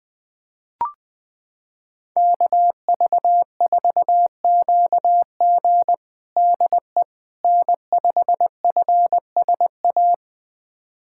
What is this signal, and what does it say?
Morse code, heard as a keyed tone, about 20 wpm, 700 Hz. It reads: KV4QG DE N5FSA
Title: USA Callsign DE Callsign - 5 Characters - 20wpm